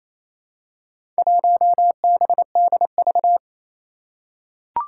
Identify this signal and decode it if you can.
Morse code, heard as a keyed tone, about 28 wpm, 700 Hz. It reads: 1BDV
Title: Four Challenging Letter-Number Combinations 28wpm